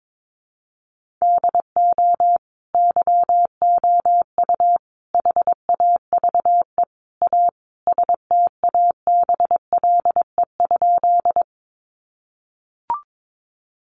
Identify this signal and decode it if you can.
Morse code, heard as a keyed tone, about 22 wpm, 700 Hz. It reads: DO YOU HAVE A STABLE?
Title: Sentences from Top 1500 Words 22wpm